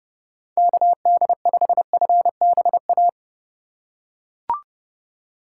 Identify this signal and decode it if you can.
Morse code, heard as a keyed tone, about 30 wpm, 700 Hz. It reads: KD5FBA
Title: K1USN SST QSO Practice - 30wpm